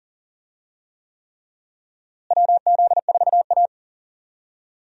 Morse code, heard as a keyed tone, about 37 wpm, 700 Hz. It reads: WZ4A
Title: USA Call Signs - 4 Characters - Repeated 3x - 40wpm